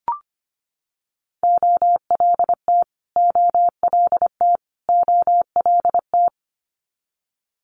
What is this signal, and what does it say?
Morse code, heard as a keyed tone, about 25 wpm, 700 Hz. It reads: OLT OLT OLT
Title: Multi-letter Phonemes Repeated 3x 25wpm